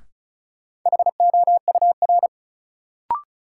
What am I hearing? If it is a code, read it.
Morse code, heard as a keyed tone, about 36 wpm, 700 Hz. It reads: HOUR
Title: Sets of 5 Words - New Word Drill - 35wpm